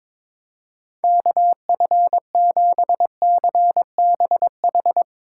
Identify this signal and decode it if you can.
Morse code, heard as a keyed tone, about 22 wpm, 700 Hz. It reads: KF7CBH
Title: Speed Racing - USA Call Signs 22wpm